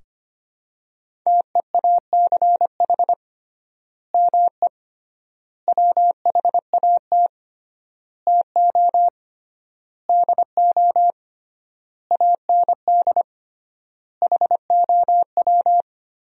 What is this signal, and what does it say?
Morse code, heard as a keyed tone, about 25 wpm, 700 Hz. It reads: TEACH ME WHAT TO DO AND HOW
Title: 3x Word-Spacing - Sentences from Top 500 Words - 25wpm